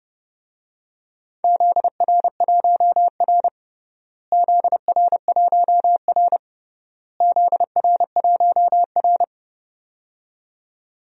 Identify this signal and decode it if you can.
Morse code, heard as a keyed tone, about 32 wpm, 700 Hz. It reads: ZR1R ZR1R ZR1R
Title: DXCC-100 Call Signs - 4 Characters - Repeated 3x - 30wpm